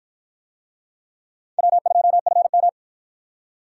Morse code, heard as a keyed tone, about 52 wpm, 700 Hz. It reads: W2FM